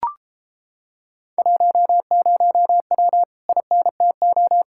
Morse code, heard as a keyed tone, about 29 wpm, 700 Hz. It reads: 10W INTO